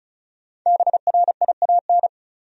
Morse code, heard as a keyed tone, about 36 wpm, 700 Hz. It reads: BRIAN
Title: USA Common Names - 35wpm